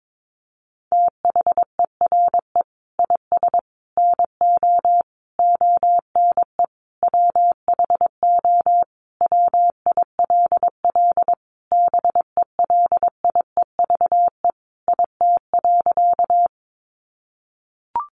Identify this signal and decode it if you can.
Morse code, heard as a keyed tone, about 22 wpm, 700 Hz. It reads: THERE IS NO ONE WHO WILL BELIEVE IT.
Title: Sentences from Top 1000 Words 22wpm